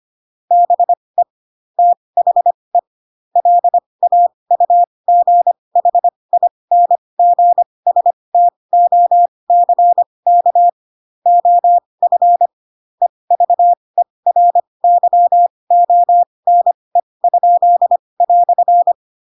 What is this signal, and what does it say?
Morse code, heard as a keyed tone, about 25 wpm, spaced as for 20 wpm, 700 Hz. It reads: BE THE LAUGHINGSTOCK OF EVERYONE?"
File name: Pinocchio / Pinocchio ch4 15 25 ik1hge